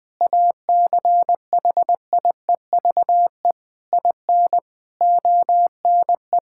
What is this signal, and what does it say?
Morse code, heard as a keyed tone, about 20 wpm, 700 Hz. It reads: ACHIEVE IN ONE
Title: Sentences from Top 1000 Words - Review Part 6 - 20wpm